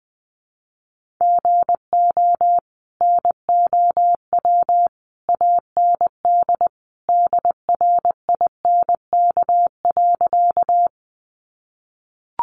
Morse code, heard as a keyed tone, about 20 wpm, 700 Hz. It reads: GO NOW AND DRINK.
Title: Sentences from Top 1500 Words 20wpm